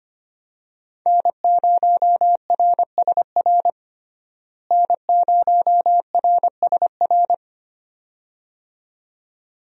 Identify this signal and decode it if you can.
Morse code, heard as a keyed tone, about 25 wpm, 700 Hz. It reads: N0RSR N0RSR
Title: USA Call Signs - 5 Characters - Repeated 2x - 25wpm